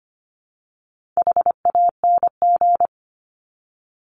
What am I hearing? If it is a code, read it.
Morse code, heard as a keyed tone, about 25 wpm, 700 Hz. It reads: HANG
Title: Top 1501-1750 Words 25wpm